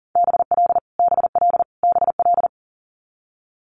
Morse code, heard as a keyed tone, about 42 wpm, 700 Hz. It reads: BL BL BL
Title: Multi-letter Phonemes Repeated 3x 40wpm